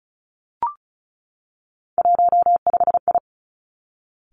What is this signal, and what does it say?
Morse code, heard as a keyed tone, about 36 wpm, 700 Hz. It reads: 15I
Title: Three Challenging Letter-Number Combinations 35wpm